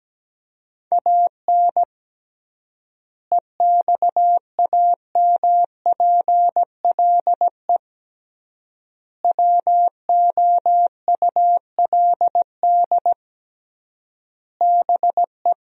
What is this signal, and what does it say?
Morse code, heard as a keyed tone, about 17 wpm, 700 Hz. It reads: AN EXAMPLE WOULD BE
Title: 3x Word-Spacing - Sentences from Top 400 Words - 17wpm